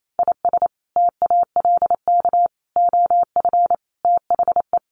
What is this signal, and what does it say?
Morse code, heard as a keyed tone, about 28 wpm, 700 Hz. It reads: IS TALK OF THE
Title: Sets of 5 Words Encore2 28wpm